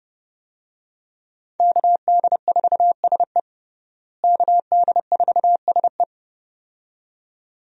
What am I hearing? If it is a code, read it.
Morse code, heard as a keyed tone, about 32 wpm, 700 Hz. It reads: KD4SE KD4SE
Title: USA Call Signs - 5 Characters - Repeated 2x - 30wpm